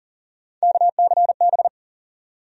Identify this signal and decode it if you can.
Morse code, heard as a keyed tone, about 40 wpm, 700 Hz. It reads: KCB
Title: USA Call Sign Postfixes - 3 Letters 40wpm